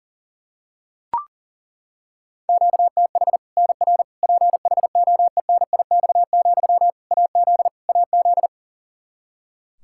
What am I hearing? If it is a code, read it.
Morse code, heard as a keyed tone, about 40 wpm, 700 Hz. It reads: QTH NR PHOENIX, AZ AZ
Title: K1USN SST QSO Practice 40wpm